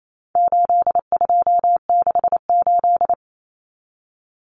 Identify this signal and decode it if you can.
Morse code, heard as a keyed tone, about 28 wpm, 700 Hz. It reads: 8268
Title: Numbers - 4-Digit 28wpm